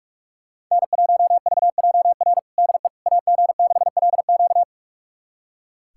Morse code, heard as a keyed tone, about 45 wpm, 700 Hz. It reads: N1VJR DE AG6LQ
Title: USA Callsign DE Callsign - 5 Characters - 45wpm